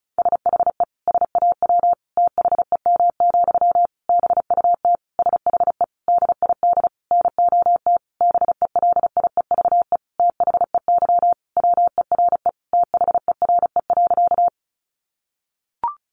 Morse code, heard as a keyed tone, about 34 wpm, 700 Hz. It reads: SHE SAW THEM, BUT SHE DID NOT BELIEVE THEY WERE THERE.
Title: Sentences from Top 1000 Words 35wpm